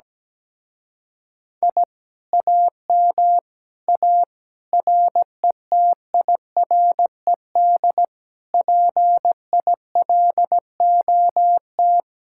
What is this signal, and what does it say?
Morse code, heard as a keyed tone, about 18 wpm, 700 Hz. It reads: I AM A RETIRED PILOT
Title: Speed-Racing Common QSO Phrases - 17wpm